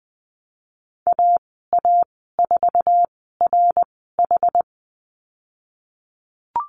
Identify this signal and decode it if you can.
Morse code, heard as a keyed tone, about 20 wpm, spaced as for 14 wpm, 700 Hz. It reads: AA4RH
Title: 128 Lesson 21g - Number 6 - Call Signs 20wpm